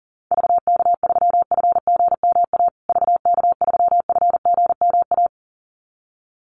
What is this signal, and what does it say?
Morse code, heard as a keyed tone, about 40 wpm, 700 Hz. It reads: VK3FGMA VK3FGMA
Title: USA Call Signs Repeated 2x 40wpm